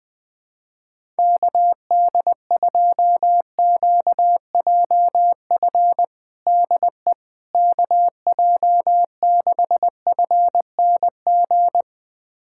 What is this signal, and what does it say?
Morse code, heard as a keyed tone, about 20 wpm, 700 Hz. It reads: KD2QJF DE KJ6FNG
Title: USA Callsign DE Callsign - 6 Characters - 20wpm